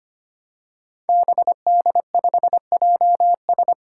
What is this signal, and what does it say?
Morse code, heard as a keyed tone, about 25 wpm, 700 Hz. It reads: BD5JS
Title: Five Challenging Letter-Number Combinations 25wpm